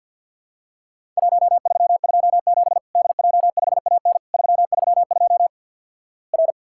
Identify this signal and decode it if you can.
Morse code, heard as a keyed tone, about 50 wpm, 700 Hz. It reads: 1327 DJ5AN 332 R
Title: Worked All Europe DX Contest - QTC Exchange with Repeats - 50wpm